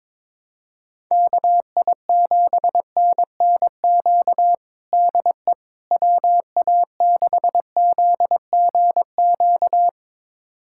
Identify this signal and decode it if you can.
Morse code, heard as a keyed tone, about 22 wpm, 700 Hz. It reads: KI7NNQ DE WA6ZGQ
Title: USA Callsign DE Callsign - 6 Characters - 22wpm